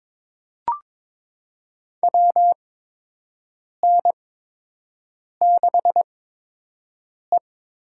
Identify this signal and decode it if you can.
Morse code, heard as a keyed tone, about 21 wpm, spaced as for 6 wpm, 700 Hz. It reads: WN6E
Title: USA Call Signs - 4 Characters - 8x Character Spacing - 22wpm